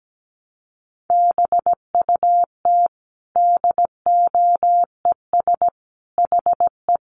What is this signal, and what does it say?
Morse code, heard as a keyed tone, about 17 wpm, 700 Hz. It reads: BUT DOES HE
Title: Sentences from Top 1200 Words 17wpm